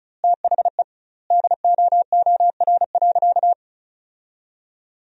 Morse code, heard as a keyed tone, about 36 wpm, 700 Hz. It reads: THE DOOR.
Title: 2x Word-Spacing - Sentences from Top 400 Words - 35wpm